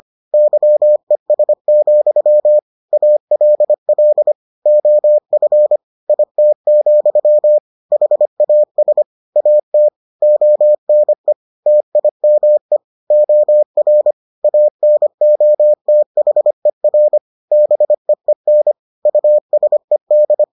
Morse code, heard as a keyed tone, about 25 wpm, 600 Hz. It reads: YES, ALL OF IT, HAS AT ONE TIME OR ANOTHER BEEN USED